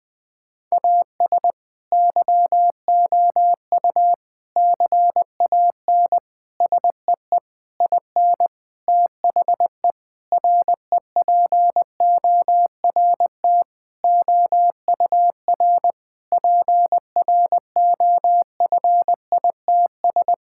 Morse code, heard as a keyed tone, about 21 wpm, 700 Hz. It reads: AS YOU CAN SEE IN THE REPORT OUR PROFITS
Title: Sentences with 6-Word Phrases - 20wpm